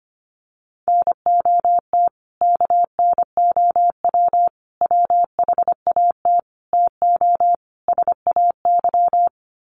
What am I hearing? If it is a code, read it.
Morse code, heard as a keyed tone, about 25 wpm, 700 Hz. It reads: NOT KNOW WHAT TO SAY
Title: Sets of 5 Words 25wpm